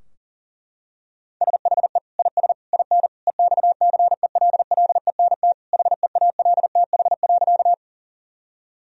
Morse code, heard as a keyed tone, about 40 wpm, 700 Hz. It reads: SHE IS IN EXCELLENT HEALTH.
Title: Sentences from Top 1000 Words - Review Part 3 - 40wpm